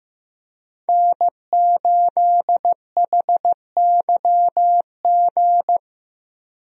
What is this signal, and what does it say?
Morse code, heard as a keyed tone, about 15 wpm, 700 Hz. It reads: N8HYG
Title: USA Call Signs - 5 Characters - Repeated 2x - 15wpm